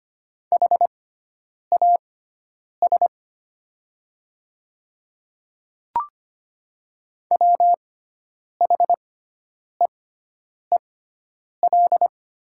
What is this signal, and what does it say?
Morse code, heard as a keyed tone, about 25 wpm, spaced as for 9 wpm, 700 Hz. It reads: HAS WHEEL
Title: 6x Character Spacing - Top 500 Words - 25wpm